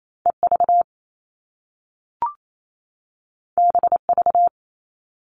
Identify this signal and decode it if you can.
Morse code, heard as a keyed tone, about 28 wpm, 700 Hz. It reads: EV BV
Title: All Cut Numbers - 2-Digit 28wpm